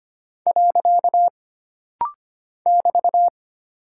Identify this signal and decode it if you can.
Morse code, heard as a keyed tone, about 25 wpm, 700 Hz. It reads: . <BT>
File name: Single Punctuation-Prosign - Rapid-Fire - 25wpm